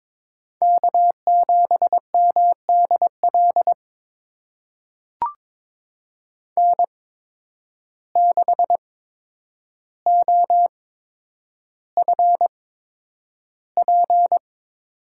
Morse code, heard as a keyed tone, about 21 wpm, spaced as for 9 wpm, 700 Hz. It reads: K7MDL N6OFP
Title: USA Call Signs - 8x Character Spacing - 22wpm